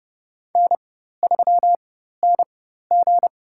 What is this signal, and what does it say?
Morse code, heard as a keyed tone, about 30 wpm, spaced as for 14 wpm, 700 Hz. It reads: N3NG
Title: USA Call Signs - 4 Characters - 4x Character Spacing - 30wpm